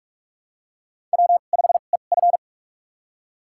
Morse code, heard as a keyed tone, about 47 wpm, spaced as for 33 wpm, 690 Hz. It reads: W5EF
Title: USA Call Signs - 4 Characters - 2x Character Spacing - 45wpm